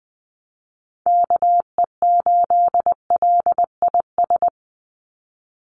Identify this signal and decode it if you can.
Morse code, heard as a keyed tone, about 20 wpm, 700 Hz. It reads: KE8LIS
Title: USA Call Signs 20wpm